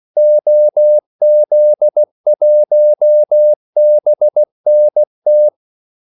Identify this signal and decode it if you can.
Morse code, heard as a keyed tone, about 16 wpm, 600 Hz. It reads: OZ1BNT
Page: Kaldesignaler 16 wpm